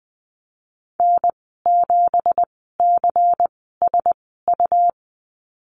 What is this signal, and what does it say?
Morse code, heard as a keyed tone, about 19 wpm, spaced as for 14 wpm, 700 Hz. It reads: N7CSU
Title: 179 Lesson 29g - Slash - Call Signs 20wpm